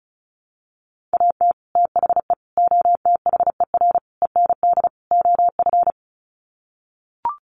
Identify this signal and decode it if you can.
Morse code, heard as a keyed tone, about 34 wpm, 700 Hz. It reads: AT THE OTHER END OF
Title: Sets of 5 Words 35wpm